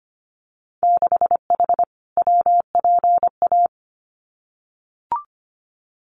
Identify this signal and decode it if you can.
Morse code, heard as a keyed tone, about 25 wpm, 700 Hz. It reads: 6H WPA
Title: Winter Field Day Exchanges Encore 25wpm